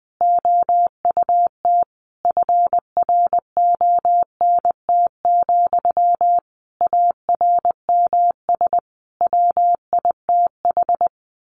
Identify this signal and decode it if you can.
Morse code, heard as a keyed tone, about 20 wpm, 700 Hz. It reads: OUT FRONT, ARMS WITH
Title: A Sound Of Thunder 20wpm